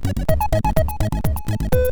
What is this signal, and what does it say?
There is no code.